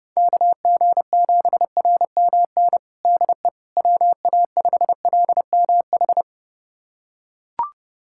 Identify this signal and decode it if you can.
Morse code, heard as a keyed tone, about 30 wpm, 700 Hz. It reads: KG7RMN DE WA5LMH
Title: USA Callsign DE Callsign - 6 Characters - 30wpm